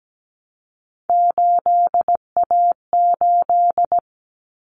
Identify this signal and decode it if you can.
Morse code, heard as a keyed tone, about 17 wpm, 700 Hz. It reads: 8A8
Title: Common Cut Numbers - 3-Digit 17wpm